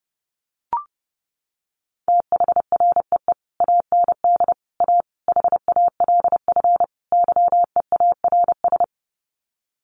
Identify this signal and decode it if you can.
Morse code, heard as a keyed tone, about 30 wpm, 700 Hz. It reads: THREE AND A HALF YEARS
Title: Sets of 5 Words Encore2 30wpm